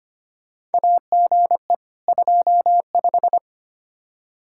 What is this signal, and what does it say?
Morse code, heard as a keyed tone, about 25 wpm, 700 Hz. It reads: AGE 25
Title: Practice QSO - Exchange 3 - More Info - 25wpm